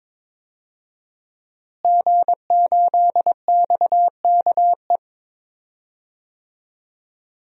Morse code, heard as a keyed tone, about 22 wpm, 700 Hz. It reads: G8XKE
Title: DXCC-100 Call Signs - 5 Characters - Repeated 2x - 22wpm